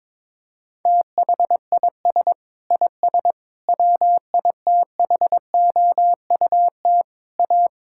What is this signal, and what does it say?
Morse code, heard as a keyed tone, about 22 wpm, 700 Hz. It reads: THIS IS WITHOUT A
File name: Sentences with 7-Word Phrases - 22wpm